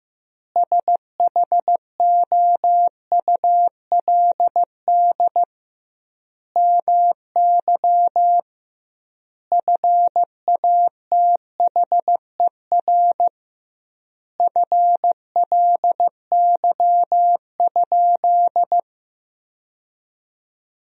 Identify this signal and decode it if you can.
Morse code, heard as a keyed tone, about 15 wpm, 700 Hz. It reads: SHOULD MY FATHER FLY?
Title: 2x Word-Spacing - Sentences from Top 500 Words - 15wpm